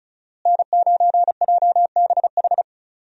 Morse code, heard as a keyed tone, about 34 wpm, 700 Hz. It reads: N9JBH
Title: USA Call Signs - 5 Characters - Repeated 2x - 35wpm